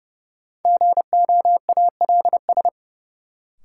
Morse code, heard as a keyed tone, about 31 wpm, 700 Hz. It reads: GOALS
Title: Sentences from Top 1000 Words - Review Part 6 - New Word Drill - 30wpm